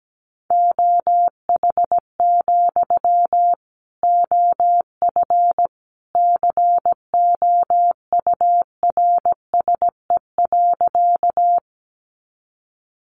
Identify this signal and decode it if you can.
Morse code, heard as a keyed tone, about 17 wpm, 700 Hz. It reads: OH, OF COURSE.
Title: Sentences from Top 500 Words 17wpm